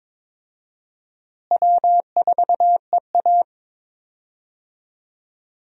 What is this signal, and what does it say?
Morse code, heard as a keyed tone, about 21 wpm, 700 Hz. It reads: W4EA
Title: USA Call Signs - 4 Characters - Repeated 3x - 22wpm